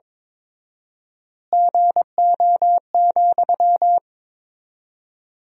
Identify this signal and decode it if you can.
Morse code, heard as a keyed tone, about 22 wpm, 700 Hz. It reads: GO,
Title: 4x Word-Spacing - Sentences from Top 400 Words - 22wpm